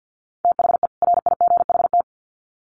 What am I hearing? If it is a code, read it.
Morse code, heard as a keyed tone, about 50 wpm, 700 Hz. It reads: THE RIGHT
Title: Sets of 2 Words 50wpm